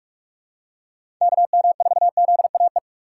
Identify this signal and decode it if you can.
Morse code, heard as a keyed tone, about 46 wpm, 700 Hz. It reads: KM4ZAE
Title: USA Call Signs - 6 Characters - 4x Character Spacing - 45wpm